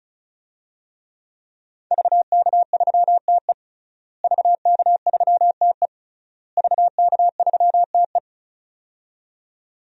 Morse code, heard as a keyed tone, about 34 wpm, 700 Hz. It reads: VK3TE VK3TE VK3TE
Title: DXCC-100 Call Signs - 5 Characters - Repeated 3x - 35wpm